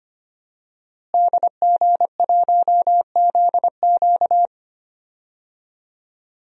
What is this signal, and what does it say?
Morse code, heard as a keyed tone, about 25 wpm, 700 Hz. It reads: DG1ZQ
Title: DXCC-100 Call Signs - 5 Characters - Repeated 2x - 25wpm